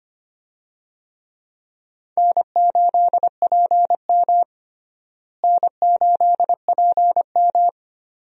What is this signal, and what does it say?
Morse code, heard as a keyed tone, about 25 wpm, 700 Hz. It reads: N8PM N8PM
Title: USA Call Signs - 4 Characters - Repeated 2x - 25wpm